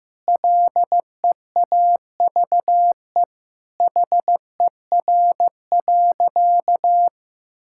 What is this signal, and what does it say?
Morse code, heard as a keyed tone, about 15 wpm, 700 Hz. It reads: LEAVE HER.
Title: Sentences from Top 1000 Words - Review Part 5 - 15wpm